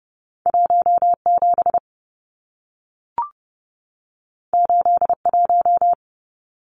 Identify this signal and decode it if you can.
Morse code, heard as a keyed tone, about 30 wpm, 700 Hz. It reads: 17 81
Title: Numbers - 2-Digit 30wpm